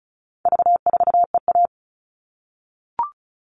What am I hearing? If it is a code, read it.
Morse code, heard as a keyed tone, about 34 wpm, 700 Hz. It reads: V4EA
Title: All Cut Numbers - 4-Digit 35wpm